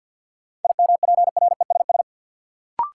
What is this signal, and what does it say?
Morse code, heard as a keyed tone, about 48 wpm, 700 Hz. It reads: IMPRESS
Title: Sentences from Top 1600 Words - New Word Drill Alt - 50wpm